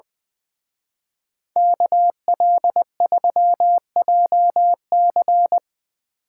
Morse code, heard as a keyed tone, about 22 wpm, 700 Hz. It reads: KL3JC
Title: Speed Racing - USA 2x2 Call Signs 20wpm